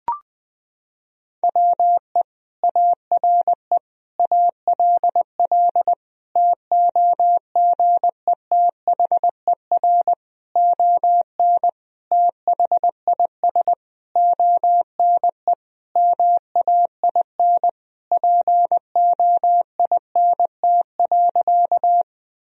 Morse code, heard as a keyed tone, about 21 wpm, 700 Hz. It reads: WE ARE ALL TOGETHER ON THIS ONE MAIN POINT.
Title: Sentences from Top 1000 Words - Review Part 2 - 20wpm